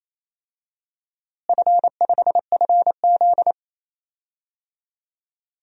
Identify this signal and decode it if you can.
Morse code, heard as a keyed tone, about 28 wpm, 700 Hz. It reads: F5FZ
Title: DXCC-100 Call Signs - 4 Characters - Repeated 2x - 28wpm